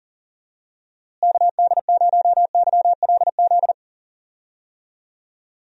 Morse code, heard as a keyed tone, about 40 wpm, 700 Hz. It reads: KD0YLZ
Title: USA Call Signs - 6 Characters - Repeated 2x - 40wpm